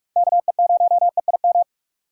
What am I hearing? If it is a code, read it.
Morse code, heard as a keyed tone, about 45 wpm, 700 Hz. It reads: KE0EIM
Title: USA Call Signs - 4x Character Spacing - 45wpm